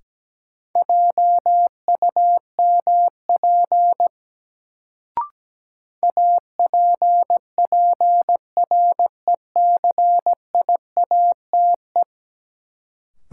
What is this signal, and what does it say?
Morse code, heard as a keyed tone, about 17 wpm, 700 Hz. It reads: JUMP APPRECIATE
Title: Sentences from Top 1300 Words - New Word Drill Alt - 17wpm